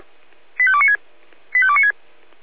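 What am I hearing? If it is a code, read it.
Five-tone selective call (ZVEI), heard as quick runs of five tones, tones of about 70 ms. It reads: 85296 85296
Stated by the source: ZVEI five-tone call received by SDR.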